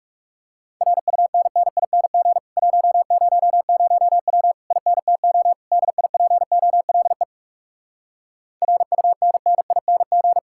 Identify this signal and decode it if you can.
Morse code, heard as a keyed tone, about 40 wpm, 700 Hz. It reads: RUNNING 100W INTO DIPOLE RUNNING